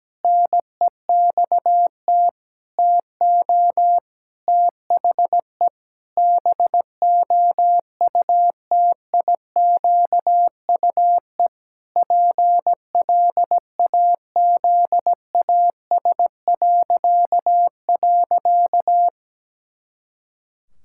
Morse code, heard as a keyed tone, about 17 wpm, 700 Hz. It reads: NEXT TO THE BOUTIQUE PLAZAS..
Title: Pangram Challenge v2 - 17wpm